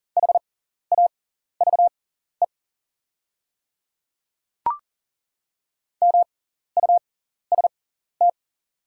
Morse code, heard as a keyed tone, about 40 wpm, spaced as for 14 wpm, 700 Hz. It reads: HAVE MUST
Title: 6x Character Spacing - Top 500 Words - 40wpm